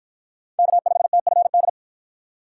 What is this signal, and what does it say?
Morse code, heard as a keyed tone, about 48 wpm, 700 Hz. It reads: K5TFD